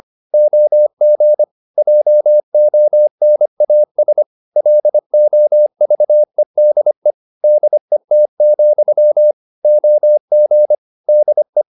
Morse code, heard as a keyed tone, about 25 wpm, 600 Hz. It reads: OG JONAS LOVEDE DET, OG DE
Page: Ferie 25 wpm